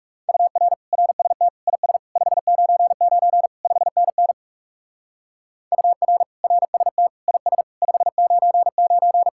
Morse code, heard as a keyed tone, about 38 wpm, 700 Hz. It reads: UR RST IS 599 5NN UR RST IS 599